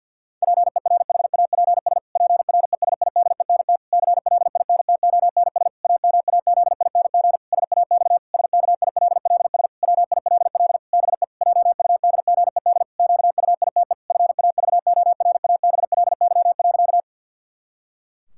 Morse code, heard as a keyed tone, about 52 wpm, 700 Hz. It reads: PERHAPS PRESIDENT CLINTONS AMAZING SAX SKILLS WILL BE JUDGED QUITE FAVORABLY.
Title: Pangram Challenge v2 - 50wpm